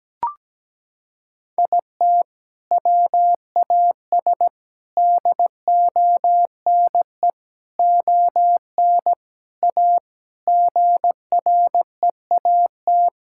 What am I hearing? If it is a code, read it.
Morse code, heard as a keyed tone, about 17 wpm, 700 Hz. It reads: IT WAS DONE ON A GREAT
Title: Sentences from Top 1000 Words - Review Part 5 - 17wpm